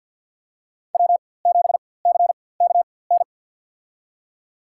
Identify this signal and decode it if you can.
Morse code, heard as a keyed tone, about 52 wpm, spaced as for 23 wpm, 700 Hz. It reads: W7CKN
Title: USA Call Signs - 5 Characters - 4x Character Spacing - 50wpm